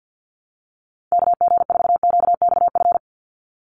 Morse code, heard as a keyed tone, about 49 wpm, 700 Hz. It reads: KG4QXF
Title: USA Call Signs 50wpm